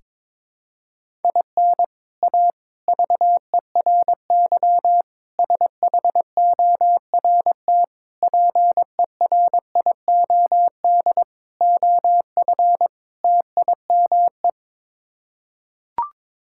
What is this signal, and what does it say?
Morse code, heard as a keyed tone, about 22 wpm, 700 Hz. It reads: IN A VERY SHORT PERIOD OF TIME
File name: practice / 22wpm / Sets of 7 Words - 22wpm